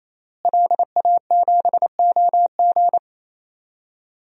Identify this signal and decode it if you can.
Morse code, heard as a keyed tone, about 28 wpm, 700 Hz. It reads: LA7OG
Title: DXCC-100 Call Signs - 5 Characters - Repeated 3x - 28wpm